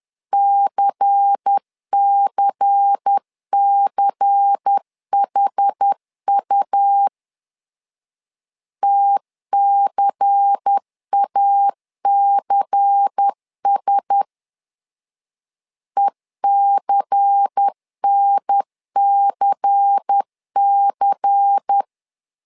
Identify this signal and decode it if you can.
Morse code, heard as a keyed tone, about 10 wpm, 800 Hz. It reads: CCCHU TCACS ECNCC
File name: LEZIONI / 5 LBVC / 5 LBVC 10wpm 2349